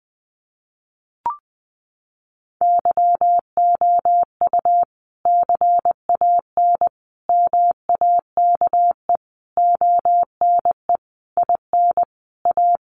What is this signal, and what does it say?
Morse code, heard as a keyed tone, about 20 wpm, 700 Hz. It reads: YOU CAN MAKE ONE IN A
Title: Sentences from Top 1000 Words 20wpm